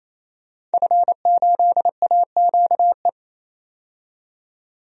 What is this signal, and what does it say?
Morse code, heard as a keyed tone, about 28 wpm, 700 Hz. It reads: F8AQE
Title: DXCC-100 Call Signs - 5 Characters - 28wpm